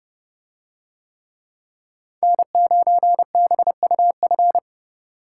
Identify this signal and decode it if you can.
Morse code, heard as a keyed tone, about 30 wpm, 700 Hz. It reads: N9BUF